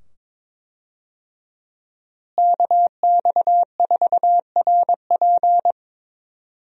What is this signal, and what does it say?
Morse code, heard as a keyed tone, about 22 wpm, 700 Hz. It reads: KX4RP
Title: USA Call Signs - 5 Characters - Repeated 2x - 22wpm